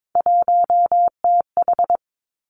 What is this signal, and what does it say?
Morse code, heard as a keyed tone, about 22 wpm, 700 Hz. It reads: 1TH
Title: Three Challenging Letter-Number Combinations 22wpm